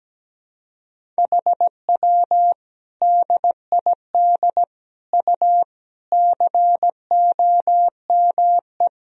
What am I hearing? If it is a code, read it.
Morse code, heard as a keyed tone, about 17 wpm, 700 Hz. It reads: HW DID U COME
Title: Sentences from Top 100 Words and Abbreviations 17wpm